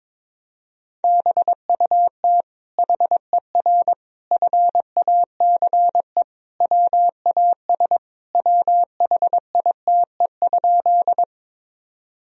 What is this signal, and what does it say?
Morse code, heard as a keyed tone, about 22 wpm, 700 Hz. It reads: BUT HER FACE WAS WHITE?
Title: Sentences from Top 1000 Words - Review Part 2 - 22wpm